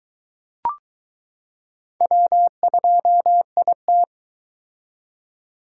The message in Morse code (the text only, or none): W2IT